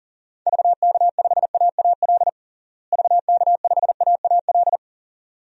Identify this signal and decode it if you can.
Morse code, heard as a keyed tone, about 37 wpm, 700 Hz. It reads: VK5AAL VK5AAL
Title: DXCC-100 Call Signs - 6 Characters - Repeated 2x - 40wpm